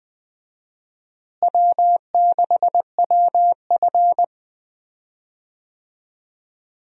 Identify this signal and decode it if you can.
Morse code, heard as a keyed tone, about 20 wpm, 700 Hz. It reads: W6WF
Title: USA Call Signs - 4 Characters - Repeated 2x - 20wpm